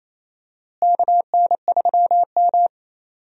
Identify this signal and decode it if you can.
Morse code, heard as a keyed tone, about 28 wpm, 700 Hz. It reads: KN3M
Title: USA Call Signs - 4 Characters - Repeated 2x - 28wpm